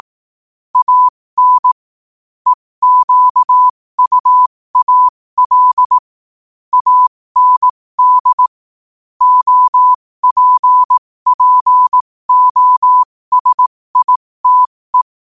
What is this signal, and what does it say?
Morse code, heard as a keyed tone, about 18 wpm, spaced as for 15 wpm, 1000 Hz. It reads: AN EQUAL AND OPPOSITE
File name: qotdfiles / 15wpm